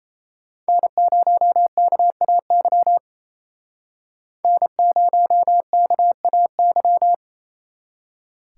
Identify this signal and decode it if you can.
Morse code, heard as a keyed tone, about 33 wpm, 700 Hz. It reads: N0KAY N0KAY
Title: Speed Racing - USA 1x3 Call Signs 28wpm